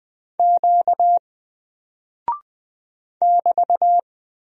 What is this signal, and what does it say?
Morse code, heard as a keyed tone, about 20 wpm, 700 Hz. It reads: Q <BT>
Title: Single Character (Letter:Number:Punctuation:Prosign) 20wpm